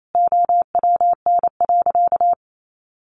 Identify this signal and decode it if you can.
Morse code, heard as a keyed tone, about 28 wpm, 700 Hz. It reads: OWN.
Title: Sentences from Top 1400 Words 28wpm